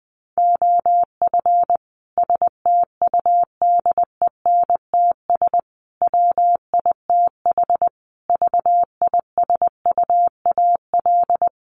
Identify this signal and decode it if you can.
Morse code, heard as a keyed tone, about 20 wpm, 700 Hz. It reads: OF STUDENTS WITH VISUAL
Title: Sets of 5 Words Encore 20wpm